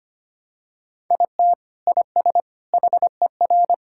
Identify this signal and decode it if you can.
Morse code, heard as a keyed tone, about 25 wpm, 700 Hz. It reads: IT IS HER
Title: Sentences from Top 1600 Words 25wpm